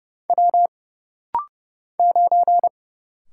Morse code, heard as a keyed tone, about 30 wpm, 700 Hz. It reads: W 9
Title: Single Letter-Number - Rapid-Fire 30wpm